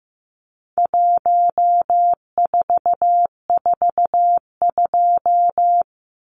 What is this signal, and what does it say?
Morse code, heard as a keyed tone, about 15 wpm, 700 Hz. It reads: 1442
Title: Numbers - 4-Digit 15wpm